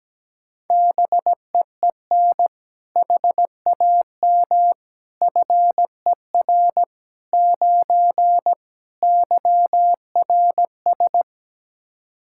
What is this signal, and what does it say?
Morse code, heard as a keyed tone, about 17 wpm, 700 Hz. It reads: BEEN HAM FER 9 YRS
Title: Practice QSO - Exchange 3 - More Info - 17wpm